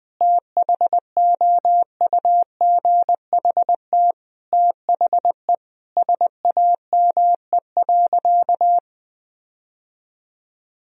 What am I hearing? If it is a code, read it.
Morse code, heard as a keyed tone, about 21 wpm, 700 Hz. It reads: THOUGHT THE SAME.
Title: Sentences from Top 1000 Words - Review Part 1 - 20wpm